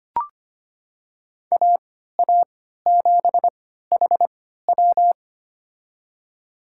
Morse code, heard as a keyed tone, about 25 wpm, spaced as for 14 wpm, 700 Hz. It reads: AA7HW
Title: USA Call Signs - 5 Characters - 3x Character Spacing - 25wpm